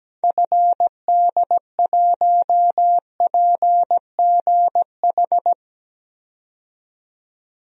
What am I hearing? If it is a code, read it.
Morse code, heard as a keyed tone, about 17 wpm, 700 Hz. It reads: FD1PGH
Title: DXCC-100 Call Signs - 6 Characters - 17wpm